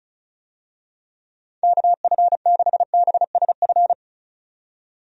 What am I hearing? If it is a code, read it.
Morse code, heard as a keyed tone, about 34 wpm, 700 Hz. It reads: KF6BSF
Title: USA Call Signs - 6 Characters - 35wpm